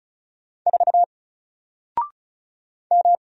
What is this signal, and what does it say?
Morse code, heard as a keyed tone, about 36 wpm, 700 Hz. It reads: 4 M
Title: Single Letter-Number Alt 35wpm